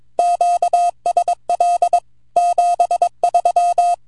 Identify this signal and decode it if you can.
Morse code, heard as a keyed tone, about 24 wpm, 700 Hz. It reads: QSL 73
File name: vq9la / VE1ZZ 10AUG08 0104Z 160mtr